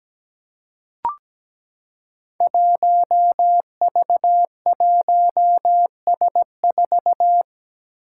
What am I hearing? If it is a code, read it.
Morse code, heard as a keyed tone, about 17 wpm, 700 Hz. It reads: 1V1S4
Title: Five Challenging Letter-Number Combinations 17wpm